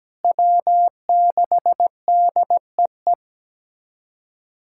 Morse code, heard as a keyed tone, about 17 wpm, 700 Hz. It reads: W6DEE
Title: USA Callsign DE Callsign - 5 Characters - 17wpm